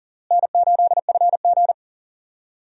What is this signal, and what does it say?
Morse code, heard as a keyed tone, about 40 wpm, 700 Hz. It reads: N8FG
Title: USA Call Signs - 4 Characters - Repeated 2x - 40wpm